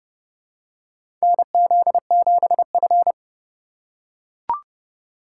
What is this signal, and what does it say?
Morse code, heard as a keyed tone, about 30 wpm, 700 Hz. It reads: NZ7F
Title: USA Call Signs - 4 Characters - 4x Character Spacing - 30wpm